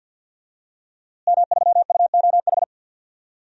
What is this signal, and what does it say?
Morse code, heard as a keyed tone, about 51 wpm, 700 Hz. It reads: M3UOH
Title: DXCC-100 Call Signs - 5 Characters - Repeated 2x - 50wpm